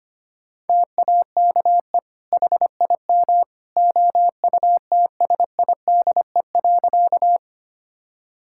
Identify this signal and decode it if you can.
Morse code, heard as a keyed tone, about 25 wpm, 700 Hz. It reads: TAKE HIM OUTSIDE.
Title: Sentences from Top 1000 Words - Review Part 6 - 25wpm